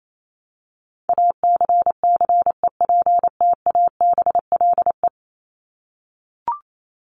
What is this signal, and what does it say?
Morse code, heard as a keyed tone, about 28 wpm, 700 Hz. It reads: ACCEPTABLE
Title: Top 1501-1750 Words 28wpm